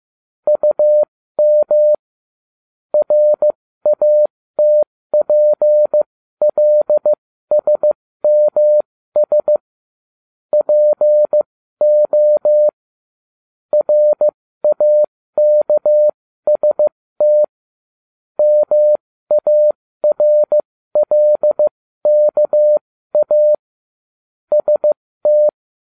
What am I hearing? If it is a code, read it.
Morse code, heard as a keyed tone, about 15 wpm, spaced as for 13 wpm, 600 Hz. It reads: UM RATPLSMS PO RAKST MARLKA ST